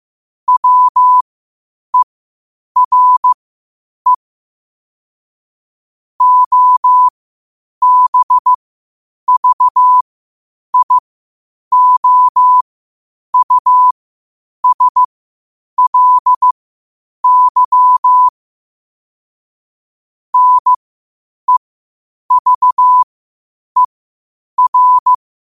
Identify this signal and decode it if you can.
Morse code, heard as a keyed tone, about 15 wpm, spaced as for 8 wpm, 1000 Hz. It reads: WERE OBVIOUSLY NEVER
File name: qotdfiles / 8wpm